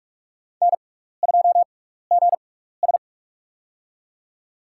Morse code, heard as a keyed tone, about 46 wpm, spaced as for 15 wpm, 700 Hz. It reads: N2GS
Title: USA Call Signs - 4 Characters - 6x Character Spacing - 45wpm